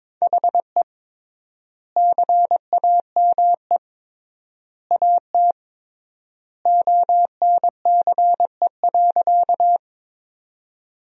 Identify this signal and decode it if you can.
Morse code, heard as a keyed tone, about 21 wpm, 700 Hz. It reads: HE CAME AT ONCE.
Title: 3x Word-Spacing - Sentences from Top 300 Words - 22wpm